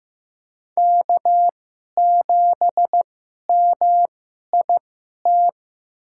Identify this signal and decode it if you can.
Morse code, heard as a keyed tone, about 15 wpm, spaced as for 11 wpm, 700 Hz. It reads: K7MIT
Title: USA Call Signs - 2x Character Spacing - 15wpm